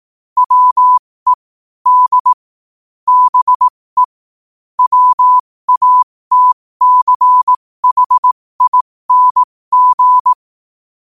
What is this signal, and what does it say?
Morse code, heard as a keyed tone, about 18 wpm, spaced as for 15 wpm, 1000 Hz. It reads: WE D BE WATCHING
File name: qotdfiles / 15wpm